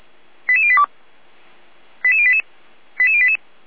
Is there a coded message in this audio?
Yes: five-tone selective call (ZVEI) from 0.5 to 3.4 s, heard as quick runs of five tones, tones of about 70 ms. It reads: 88082 88088 88088